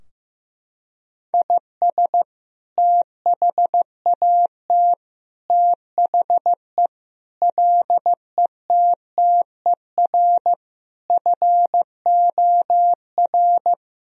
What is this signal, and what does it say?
Morse code, heard as a keyed tone, about 15 wpm, 700 Hz. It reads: IS THAT THE LETTER FOR
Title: Sentences from Top 1000 Words - Review Part 2 - 15wpm